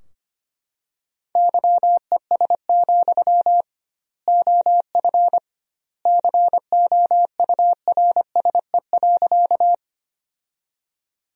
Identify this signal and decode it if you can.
Morse code, heard as a keyed tone, about 25 wpm, 700 Hz. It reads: YES, OF COURSE.
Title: 2x Word-Spacing - Sentences from Top 500 Words - 25wpm